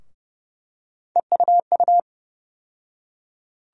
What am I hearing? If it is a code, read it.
Morse code, heard as a keyed tone, about 30 wpm, 700 Hz. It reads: EUU
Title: USA Call Sign Postfixes - 3 Letters 30wpm